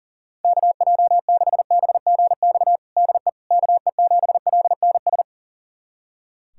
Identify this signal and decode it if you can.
Morse code, heard as a keyed tone, about 40 wpm, 700 Hz. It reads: KJ6BGX DE KE7LNS
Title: USA Callsign DE Callsign - 6 Characters - 40wpm